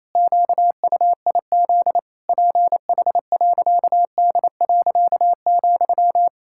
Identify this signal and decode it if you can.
Morse code, heard as a keyed tone, about 28 wpm, 700 Hz. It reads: QUIZ PH.D.,
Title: Pangram Challenge v2 - 28wpm